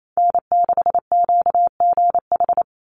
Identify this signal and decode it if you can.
Morse code, heard as a keyed tone, about 28 wpm, 700 Hz. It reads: N6QGH
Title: USA Call Signs 28wpm